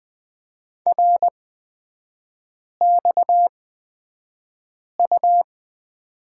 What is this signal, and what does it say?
Morse code, heard as a keyed tone, about 20 wpm, spaced as for 5 wpm, 700 Hz. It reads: RXU
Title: 236 Lesson 38e - Letter q - Three Characters 20wpm